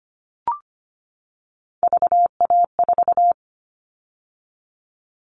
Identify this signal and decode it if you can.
Morse code, heard as a keyed tone, about 25 wpm, 700 Hz. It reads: VA4
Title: All Cut Numbers - 3-Digit 25wpm